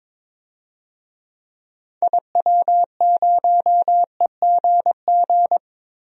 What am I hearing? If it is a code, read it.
Morse code, heard as a keyed tone, about 23 wpm, 700 Hz. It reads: IW0EGG
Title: DXCC-100 Call Signs - 6 Characters - Repeated 2x - 22wpm